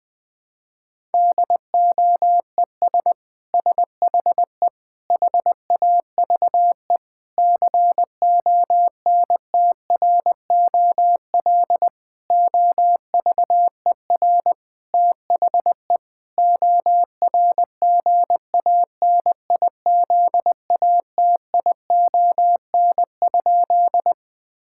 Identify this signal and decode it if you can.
Morse code, heard as a keyed tone, about 21 wpm, 700 Hz. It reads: DOES SHE HAVE CONTROL OVER THE ORGANIZATION?
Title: Sentences from Top 1000 Words - Review Part 5 - 20wpm